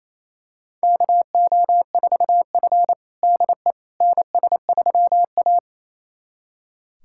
Morse code, heard as a keyed tone, about 27 wpm, 700 Hz. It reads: KO4F DE NS3A
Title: USA Callsign DE Callsign - 4 Characters - 28wpm